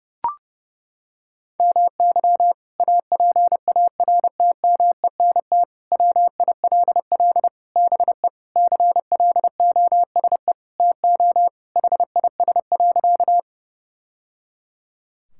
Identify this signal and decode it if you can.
Morse code, heard as a keyed tone, about 30 wpm, 700 Hz. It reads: MY APARTMENT WILL BE CLOSE TO HIS.
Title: Sentences from Top 1000 Words - Review Part 6 - 30wpm